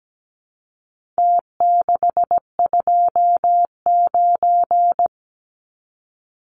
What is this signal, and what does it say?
Morse code, heard as a keyed tone, about 17 wpm, 700 Hz. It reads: T629
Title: Numbers - Contest Style - 4-Digit 17wpm